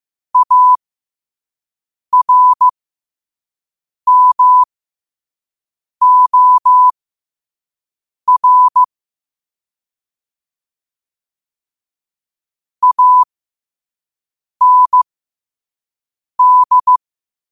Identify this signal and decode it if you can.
Morse code, heard as a keyed tone, about 15 wpm, spaced as for 5 wpm, 1000 Hz. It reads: ARMOR AND
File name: qotdfiles / 5wpm